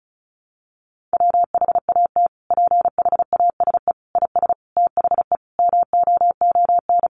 Morse code, heard as a keyed tone, about 34 wpm, 700 Hz. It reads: WHAT PHASE IS THE MOON
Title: Sentences from Top 1200 Words 35wpm